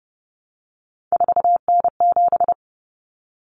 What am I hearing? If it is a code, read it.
Morse code, heard as a keyed tone, about 30 wpm, 700 Hz. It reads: 4N7
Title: Common Cut Numbers - 3-Digit 30wpm